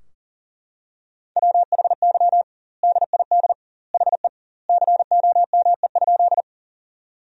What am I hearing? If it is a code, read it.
Morse code, heard as a keyed tone, about 40 wpm, 700 Hz. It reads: WHY DID HE COME?
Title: 2x Word-Spacing - Sentences from Top 200 Words - 40wpm